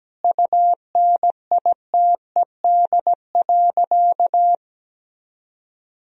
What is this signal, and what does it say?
Morse code, heard as a keyed tone, about 17 wpm, 700 Hz. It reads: UNITED.
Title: Sentences from Top 1000 Words - Review Part 4 - 17wpm